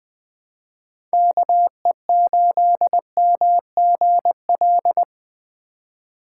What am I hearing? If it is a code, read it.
Morse code, heard as a keyed tone, about 20 wpm, 700 Hz. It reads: KE8MGL
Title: USA Call Signs - 3x Character Spacing - 20wpm